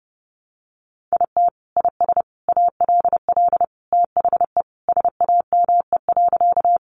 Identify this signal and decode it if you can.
Morse code, heard as a keyed tone, about 30 wpm, 700 Hz. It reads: IT IS ALL THE SAME.
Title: Sentences from Top 200 Words 30wpm